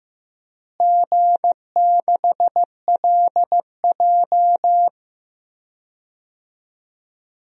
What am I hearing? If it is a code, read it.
Morse code, heard as a keyed tone, about 16 wpm, 700 Hz. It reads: G6LJ
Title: DXCC-100 Call Signs - 4 Characters - Repeated 3x - 15wpm